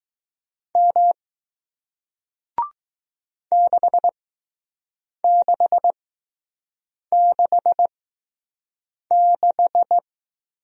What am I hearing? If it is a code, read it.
Morse code, heard as a keyed tone, about 21 wpm, spaced as for 6 wpm, 700 Hz. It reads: M 6666